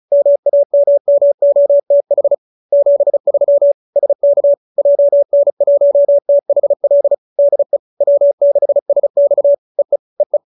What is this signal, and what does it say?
Morse code, heard as a keyed tone, about 34 wpm, 570 Hz. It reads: MAMMOTH 73 SK JN1THL DE W6SX EE EE